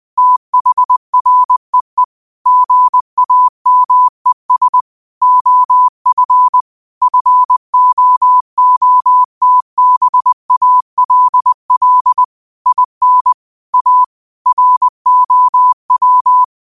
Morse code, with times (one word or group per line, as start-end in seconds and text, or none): 0.2-2.0 THREE
2.5-4.8 GAMES
5.2-6.6 OF
7.0-12.2 FOOTBALL
12.7-13.3 IN
13.7-14.0 A
14.5-16.4 ROW